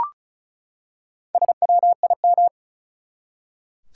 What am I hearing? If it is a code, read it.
Morse code, heard as a keyed tone, about 36 wpm, 700 Hz. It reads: SWIM
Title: Four Letter Words - 35wpm